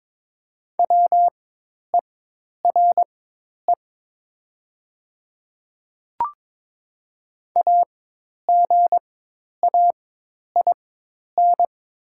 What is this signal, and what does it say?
Morse code, heard as a keyed tone, about 22 wpm, spaced as for 10 wpm, 700 Hz. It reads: WERE AGAIN
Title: 4x Character Spacing - Top 200 Words - 22wpm